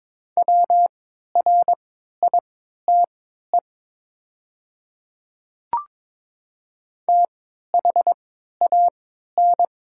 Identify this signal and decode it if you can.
Morse code, heard as a keyed tone, about 22 wpm, spaced as for 12 wpm, 700 Hz. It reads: WRITE THAN
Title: 3x Character Spacing - Top 100 Words - 22wpm